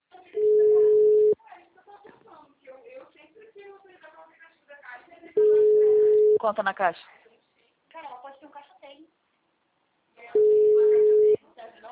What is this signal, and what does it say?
Ringback tone (Europe, 425 Hz), with speech.